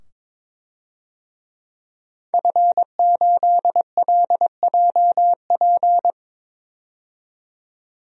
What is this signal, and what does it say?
Morse code, heard as a keyed tone, about 22 wpm, 700 Hz. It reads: F8LJP
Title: DXCC-100 Call Signs - 5 Characters - 22wpm